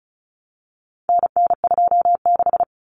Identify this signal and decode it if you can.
Morse code, heard as a keyed tone, about 34 wpm, 700 Hz. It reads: NN26
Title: Common Cut Numbers - 4-Digit 35wpm